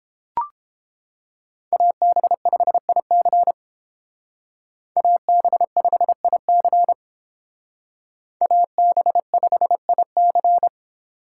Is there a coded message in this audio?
Yes: Morse code from 1.7 to 10.7 s, heard as a keyed tone, about 30 wpm, 700 Hz. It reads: AB5IC AB5IC AB5IC